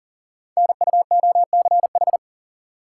Morse code, heard as a keyed tone, about 42 wpm, 690 Hz. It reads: NUOCH
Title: Code Groups - 5 Letters - 40wpm